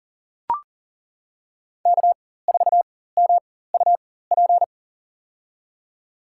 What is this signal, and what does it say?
Morse code, heard as a keyed tone, about 40 wpm, spaced as for 19 wpm, 700 Hz. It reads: K4MUP
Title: USA Call Signs - 5 Characters - 4x Character Spacing - 40wpm